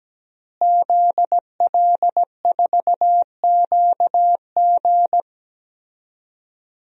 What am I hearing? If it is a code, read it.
Morse code, heard as a keyed tone, about 17 wpm, 700 Hz. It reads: ZL4QG
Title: DXCC-100 Call Signs - 5 Characters - Repeated 2x - 17wpm